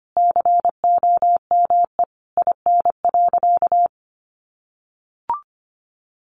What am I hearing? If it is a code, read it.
Morse code, heard as a keyed tone, about 25 wpm, 700 Hz. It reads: COME IN.
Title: Sentences from Top 1000 Words 25wpm